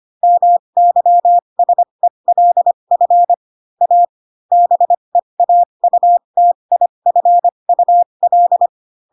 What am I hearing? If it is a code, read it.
Morse code, heard as a keyed tone, about 25 wpm, spaced as for 22 wpm, 700 Hz. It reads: MYSELF A BEAUTIFUL
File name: Pinocchio / Pinocchio ch2 18 25 ik1hge